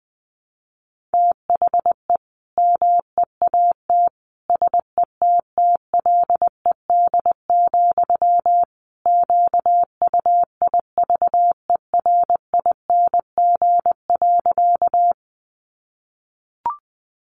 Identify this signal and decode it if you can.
Morse code, heard as a keyed tone, about 20 wpm, 700 Hz. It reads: THE MEAT SETTLED, QUIVERING.
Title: A Sound Of Thunder 20wpm